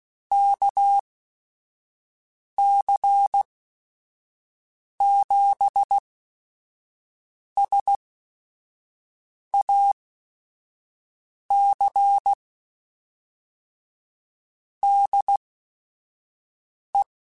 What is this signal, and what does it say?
Morse code, heard as a keyed tone, about 16 wpm, spaced as for 5 wpm, 800 Hz. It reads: KC7SAC DE